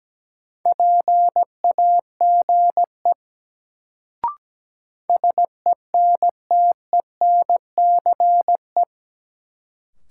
Morse code, heard as a keyed tone, about 17 wpm, 700 Hz. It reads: PAGE SENTENCE
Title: Sentences from Top 1000 Words - Review Part 1 - New Word Drill - 17wpm